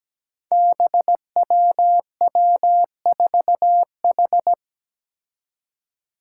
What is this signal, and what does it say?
Morse code, heard as a keyed tone, about 17 wpm, 700 Hz. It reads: BWW4H
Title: Five Challenging Letter-Number Combinations 17wpm